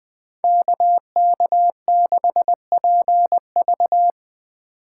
Morse code, heard as a keyed tone, about 22 wpm, 700 Hz. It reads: KK6PV
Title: Speed Racing - USA 2x2 Call Signs 20wpm